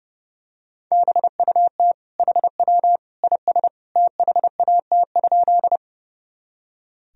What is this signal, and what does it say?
Morse code, heard as a keyed tone, about 30 wpm, 700 Hz. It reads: BUT HW IS THAT?
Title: Sentences from Top 100 Words and Abbreviations 30wpm